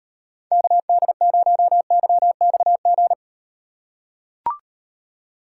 Morse code, heard as a keyed tone, about 33 wpm, 700 Hz. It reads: KD0YXG